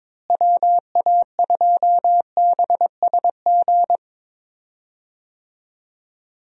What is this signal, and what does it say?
Morse code, heard as a keyed tone, about 22 wpm, 700 Hz. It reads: WA2BSG
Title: USA Call Signs - 6 Characters - Repeated 2x - 22wpm